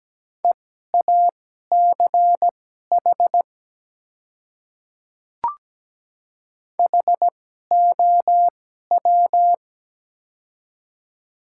Morse code, heard as a keyed tone, about 17 wpm, spaced as for 12 wpm, 700 Hz. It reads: EACH HOW
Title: 2x Character Spacing - Top 200 Words - 17wpm